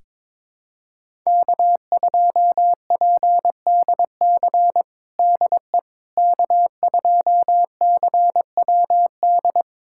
Morse code, heard as a keyed tone, about 22 wpm, 700 Hz. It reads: K2PDC DE K2CWD
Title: USA Callsign DE Callsign - 5 Characters - 22wpm